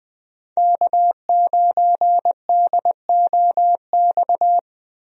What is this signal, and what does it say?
Morse code, heard as a keyed tone, about 20 wpm, 700 Hz. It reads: K9DOX
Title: USA Call Signs - 5 Characters - 20wpm